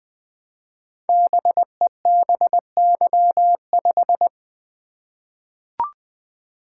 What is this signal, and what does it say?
Morse code, heard as a keyed tone, about 20 wpm, 700 Hz. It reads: BEBY5
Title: Code Groups - 5 Letter:Numbers - 20wpm